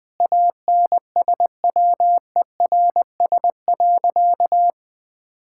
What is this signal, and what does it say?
Morse code, heard as a keyed tone, about 21 wpm, 700 Hz. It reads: ANSWERS.
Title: Sentences with 5-Word Phrases - 20wpm